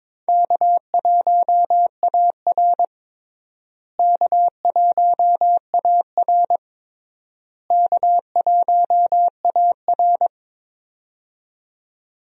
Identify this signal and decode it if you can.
Morse code, heard as a keyed tone, about 23 wpm, 700 Hz. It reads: K1AR K1AR K1AR
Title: USA Call Signs - 4 Characters - Repeated 3x - 22wpm